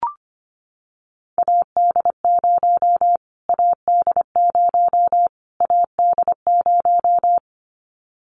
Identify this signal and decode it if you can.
Morse code, heard as a keyed tone, about 25 wpm, 700 Hz. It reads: AD0 AD0 AD0
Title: Callsign Prefixes for Top 25 DXCC - Repeated 3x - 25wpm